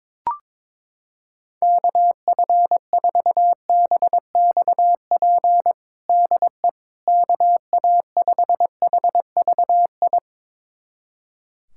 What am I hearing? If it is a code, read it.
Morse code, heard as a keyed tone, about 22 wpm, 700 Hz. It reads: KF4BXP DE KA5HVI
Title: USA Callsign DE Callsign - 6 Characters - 22wpm